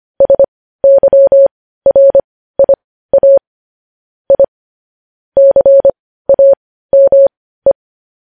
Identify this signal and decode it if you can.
Morse code, heard as a keyed tone, about 25 wpm, spaced as for 15 wpm, 550 Hz. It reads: SYRIA I CAME